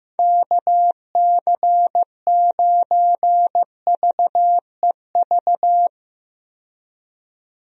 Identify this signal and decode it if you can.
Morse code, heard as a keyed tone, about 15 wpm, 700 Hz. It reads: KC9VEV
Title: USA Call Signs - 6 Characters - 15wpm